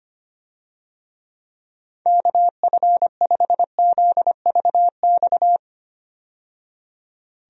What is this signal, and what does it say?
Morse code, heard as a keyed tone, about 25 wpm, 700 Hz. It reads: KF5ZVX
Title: USA Call Signs - 6 Characters - 25wpm